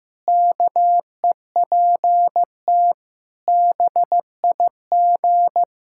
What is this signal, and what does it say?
Morse code, heard as a keyed tone, about 16 wpm, 700 Hz. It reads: KEPT BIG
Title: Pangram Challenge v2 - 15wpm